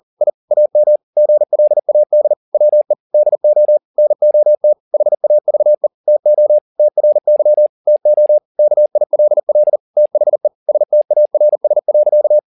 Morse code, heard as a keyed tone, about 40 wpm, 600 Hz. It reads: I AM GLAD WE DO NOT HAVE TO TRY TO KILL THE STARS.
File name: the old man and the sea 2 40wpm